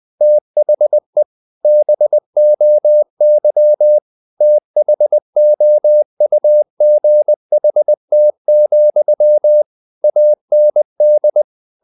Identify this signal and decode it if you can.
Morse code, heard as a keyed tone, about 20 wpm, 600 Hz. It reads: THE BOY THOUGHT, AND
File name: the old man and the sea / the old man and the sea part 1 20wpm